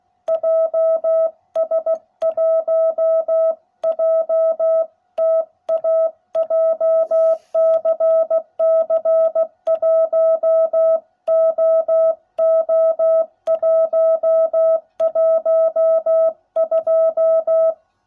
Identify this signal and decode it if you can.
Morse code, heard as a keyed tone, about 16 wpm, 660 Hz. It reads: JS1JTA JCC 1OO112